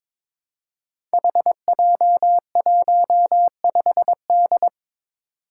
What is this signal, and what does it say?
Morse code, heard as a keyed tone, about 22 wpm, 700 Hz. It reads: HJ15D
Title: Five Challenging Letter-Number Combinations 22wpm